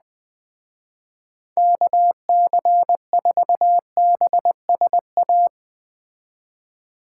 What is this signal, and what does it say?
Morse code, heard as a keyed tone, about 20 wpm, 700 Hz. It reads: KC4BSA
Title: Speed Racing - USA Call Signs 17wpm